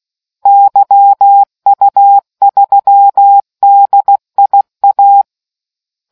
Morse code, heard as a keyed tone, about 16 wpm, 800 Hz. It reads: YU3DIA